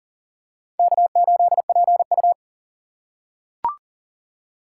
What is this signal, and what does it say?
Morse code, heard as a keyed tone, about 40 wpm, 700 Hz. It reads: K8PU
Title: USA Call Signs - 4 Characters - 3x Character Spacing - 40wpm